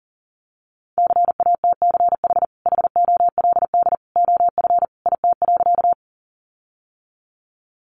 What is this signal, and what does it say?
Morse code, heard as a keyed tone, about 40 wpm, 700 Hz. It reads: CATCH HOLD OF IT.
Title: Sentences from Top 1500 Words 40wpm